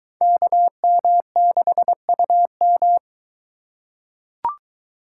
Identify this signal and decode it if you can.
Morse code, heard as a keyed tone, about 21 wpm, 700 Hz. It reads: KM6UM